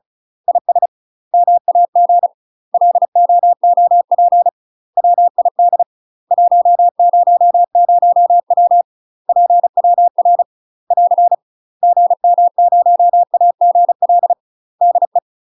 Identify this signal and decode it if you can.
Morse code, heard as a keyed tone, about 34 wpm, 700 Hz. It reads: IS MAG LOOP WID 100W PWR <AR> GM0AGL DE
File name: UK MASTER QSOs TRAINING/35WPM